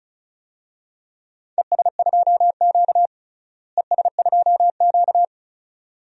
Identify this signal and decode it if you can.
Morse code, heard as a keyed tone, about 34 wpm, 700 Hz. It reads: ES2Q ES2Q
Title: DXCC-100 Call Signs - 4 Characters - Repeated 2x - 35wpm